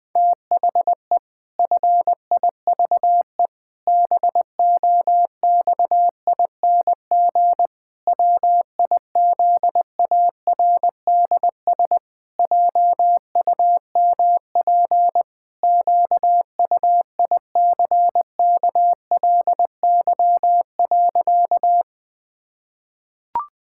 Morse code, heard as a keyed tone, about 21 wpm, 700 Hz. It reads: THE FIVE BOXING WIZARDS JUMP QUICKLY.
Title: Pangram Challenge v2 - 20wpm